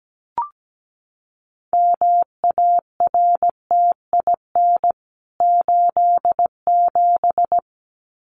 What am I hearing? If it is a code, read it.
Morse code, heard as a keyed tone, about 17 wpm, 700 Hz. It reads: MARTIN 87
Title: CWT Exchange 17wpm